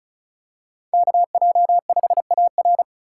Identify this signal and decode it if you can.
Morse code, heard as a keyed tone, about 36 wpm, 700 Hz. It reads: KJ5AR
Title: USA Call Signs - 8x Character Spacing - 35wpm